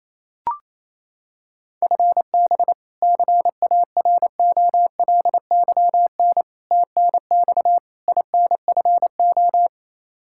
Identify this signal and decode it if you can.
Morse code, heard as a keyed tone, about 28 wpm, 700 Hz. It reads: FB CAROLYN TNX INFO
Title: Practice QSO - Exchange 3 - More Info - 28wpm